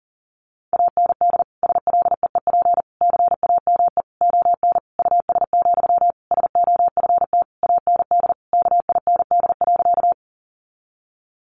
Morse code, heard as a keyed tone, about 40 wpm, 700 Hz. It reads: AND SLEEP CAME ON US, SOFT AND KIND.
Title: Sentences from Top 1500 Words 40wpm